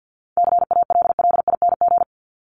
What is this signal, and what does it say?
Morse code, heard as a keyed tone, about 50 wpm, 700 Hz. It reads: CALLING
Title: ING Suffix - Top 100 Words with Base Words 50wpm